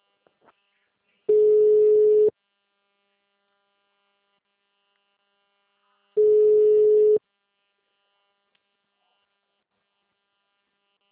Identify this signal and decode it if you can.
Ringback tone (Europe, 425 Hz).